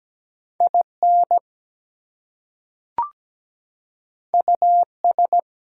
Morse code, heard as a keyed tone, about 17 wpm, 700 Hz. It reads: IN US
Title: Two Letter Words - 17wpm